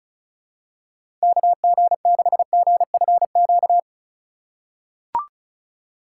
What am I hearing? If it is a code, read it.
Morse code, heard as a keyed tone, about 34 wpm, 700 Hz. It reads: KG6GFQ
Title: USA Call Signs - 6 Characters - 2x Character Spacing - 35wpm